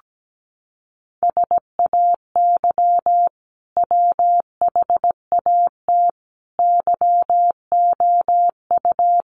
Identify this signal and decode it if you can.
Morse code, heard as a keyed tone, about 17 wpm, 700 Hz. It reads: SAY WHAT YOU
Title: Sentences from Top 200 Words 17wpm